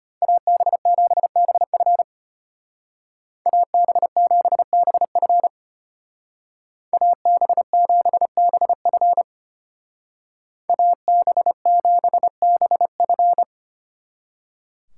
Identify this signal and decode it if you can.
Morse code, heard as a keyed tone, about 33 wpm, 700 Hz. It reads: AB7BF AB7BF AB7BF AB7BF